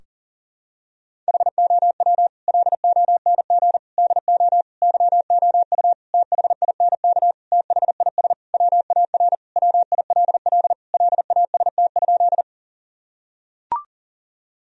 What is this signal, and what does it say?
Morse code, heard as a keyed tone, about 40 wpm, 700 Hz. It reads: HOW LONG DO YOU THINK THIS WAR WILL LAST?
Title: Sentences from Top 1000 Words - Review Part 2 - 40wpm